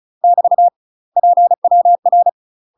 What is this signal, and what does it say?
Morse code, heard as a keyed tone, about 33 wpm, 700 Hz. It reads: <BT> PWR
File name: UK MASTER QSOs TRAINING / 35WPM / UK MASTER QSOs 0041